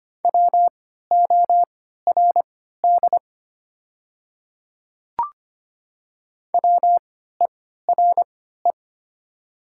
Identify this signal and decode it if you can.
Morse code, heard as a keyed tone, about 25 wpm, spaced as for 14 wpm, 700 Hz. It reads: WORD WERE
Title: 3x Character Spacing - Top 100 Words - 25wpm